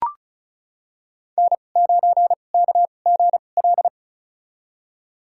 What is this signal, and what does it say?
Morse code, heard as a keyed tone, about 36 wpm, spaced as for 25 wpm, 700 Hz. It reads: N9KGL
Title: USA Call Signs - 2x Character Spacing - 35wpm